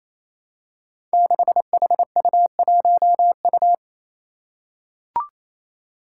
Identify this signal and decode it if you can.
Morse code, heard as a keyed tone, about 28 wpm, 700 Hz. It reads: 6HU1U
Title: Five Challenging Letter-Number Combinations 28wpm